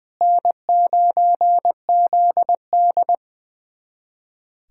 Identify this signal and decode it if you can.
Morse code, heard as a keyed tone, about 20 wpm, 700 Hz. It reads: N9ZD
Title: USA Callsign DE Callsign - 4 Characters - 20wpm